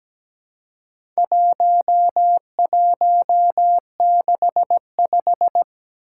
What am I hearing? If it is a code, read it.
Morse code, heard as a keyed tone, about 17 wpm, 700 Hz. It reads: 1165
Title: Four Challenging Letter-Number Combinations 17wpm